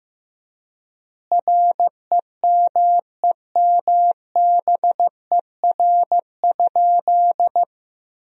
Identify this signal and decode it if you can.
Morse code, heard as a keyed tone, about 16 wpm, 700 Hz. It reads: REMEMBER?
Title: 3x Word-Spacing - Sentences from Top 400 Words - 15wpm